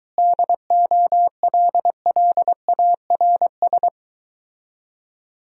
Morse code, heard as a keyed tone, about 24 wpm, 700 Hz. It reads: DOLLARS